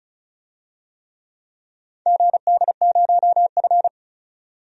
Morse code, heard as a keyed tone, about 36 wpm, 700 Hz. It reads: GD0F
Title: DXCC-100 Call Signs - 4 Characters - 35wpm